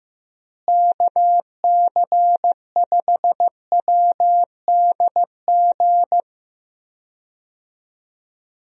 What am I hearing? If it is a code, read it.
Morse code, heard as a keyed tone, about 14 wpm, 700 Hz. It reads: KC5WDG
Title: USA Call Signs - 6 Characters - Repeated 3x - 15wpm